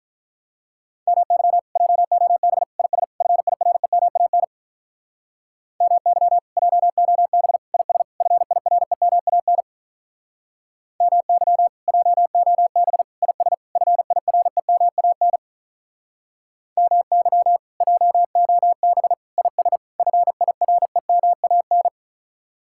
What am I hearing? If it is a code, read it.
Morse code, heard as a keyed tone, about 44 wpm, 700 Hz. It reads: MY JOB IS FIREMAN MY JOB IS FIREMAN MY JOB IS FIREMAN MY JOB IS FIREMAN